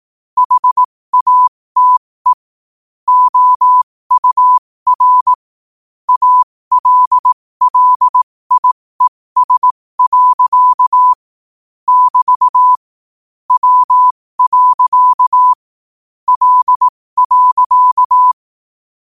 Morse code, heard as a keyed tone, about 18 wpm, spaced as for 15 wpm, 1000 Hz. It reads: HATE OUR ALLIES. <BT> W. L.
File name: qotdfiles / 15wpm